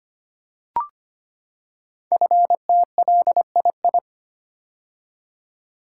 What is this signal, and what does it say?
Morse code, heard as a keyed tone, about 25 wpm, 700 Hz. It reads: FTLII
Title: Code Groups - 5 Letters - 25wpm